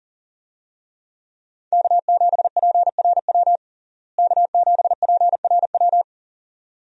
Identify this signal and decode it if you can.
Morse code, heard as a keyed tone, about 40 wpm, 700 Hz. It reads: K7PRW K7PRW
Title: USA Call Signs - 5 Characters - Repeated 2x - 40wpm